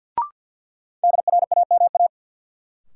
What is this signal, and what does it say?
Morse code, heard as a keyed tone, about 53 wpm, 700 Hz. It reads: DRAMA
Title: Sentences from Top 1500 Words - New Word Drill Alt - 50wpm